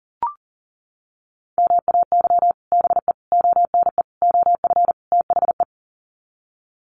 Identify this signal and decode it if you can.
Morse code, heard as a keyed tone, about 40 wpm, 700 Hz. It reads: MAY BE ONE OF THE
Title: Sets of 5 Words Encore2 40wpm